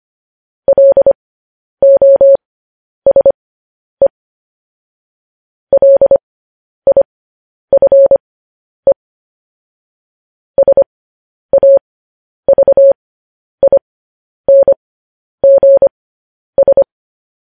Morse code, heard as a keyed tone, about 25 wpm, spaced as for 10 wpm, 550 Hz. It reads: LOSE LIFE SAVINGS
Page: news-headlines-in-morse-code-at-10-wpm